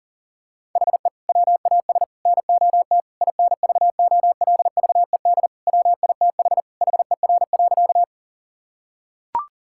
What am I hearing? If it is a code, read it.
Morse code, heard as a keyed tone, about 40 wpm, 700 Hz. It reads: HE WAS NOT INVOLVED WITH HER.
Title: Sentences from Top 1000 Words - Review Part 5 - 40wpm